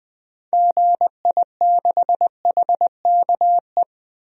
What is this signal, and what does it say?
Morse code, heard as a keyed tone, about 20 wpm, 700 Hz. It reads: GI6HKE
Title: DXCC-100 Call Signs - 6 Characters - 20wpm